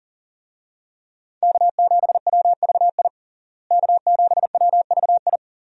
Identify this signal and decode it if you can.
Morse code, heard as a keyed tone, about 40 wpm, 700 Hz. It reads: K7WVI K7WVI
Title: USA Call Signs - 5 Characters - Repeated 2x - 40wpm